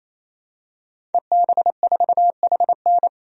Morse code, heard as a keyed tone, about 28 wpm, 700 Hz. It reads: EB4HN
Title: DXCC-100 Call Signs - 5 Characters - 28wpm